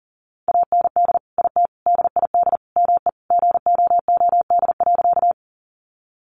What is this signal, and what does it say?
Morse code, heard as a keyed tone, about 40 wpm, 700 Hz. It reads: AND IT DID ME GOOD.
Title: Sentences from Top 200 Words 40wpm